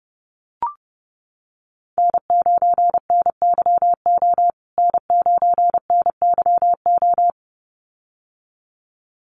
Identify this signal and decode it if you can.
Morse code, heard as a keyed tone, about 30 wpm, 700 Hz. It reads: N9NYO N9NYO
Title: USA Call Signs Repeated 2x 30wpm